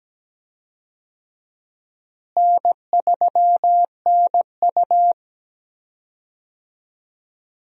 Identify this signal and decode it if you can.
Morse code, heard as a keyed tone, about 17 wpm, 700 Hz. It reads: N3NU
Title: USA Call Signs - 4 Characters - Repeated 3x - 17wpm